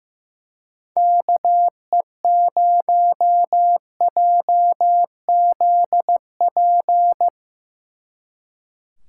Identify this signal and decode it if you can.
Morse code, heard as a keyed tone, about 16 wpm, 700 Hz. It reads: KE0JZP
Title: Speed Racing - USA 2x3 Call Signs 15wpm